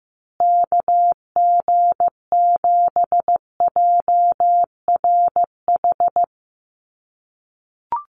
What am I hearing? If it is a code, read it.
Morse code, heard as a keyed tone, about 15 wpm, 700 Hz. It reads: KG7JRH
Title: USA Call Signs Repeated 2x 15wpm